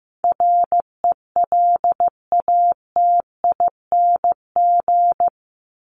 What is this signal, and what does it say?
Morse code, heard as a keyed tone, about 15 wpm, 700 Hz. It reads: RELATING
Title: ING Suffix - Top 100 Words with Base Words 15wpm